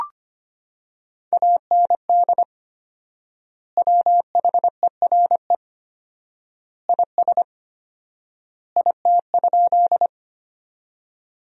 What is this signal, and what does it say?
Morse code, heard as a keyed tone, about 25 wpm, 700 Hz. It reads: AND WHERE IS IT?
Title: 4x Word-Spacing - Sentences from Top 200 Words - 25wpm